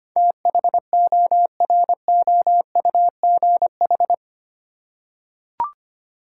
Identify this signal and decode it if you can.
Morse code, heard as a keyed tone, about 25 wpm, 700 Hz. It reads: THOROUGH
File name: 8 Letter Words - First 3 Letters Given - 25wpm